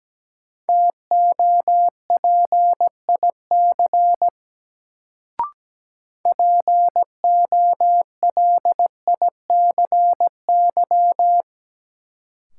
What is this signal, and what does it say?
Morse code, heard as a keyed tone, about 17 wpm, 700 Hz. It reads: TOPIC POLICY
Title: Sentences from Top 1200 Words - New Word Drill Alt - 17wpm